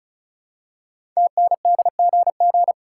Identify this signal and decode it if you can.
Morse code, heard as a keyed tone, about 36 wpm, 690 Hz. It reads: TNDGG
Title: Code Groups - 5 Letters - 35wpm